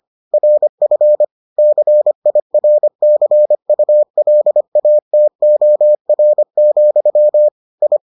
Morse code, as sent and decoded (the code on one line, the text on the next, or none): .-. ..-. / -.-. .. .-. -.-. ..- .-.. .- - --- .-. --..-- / ..
RF CIRCULATOR, I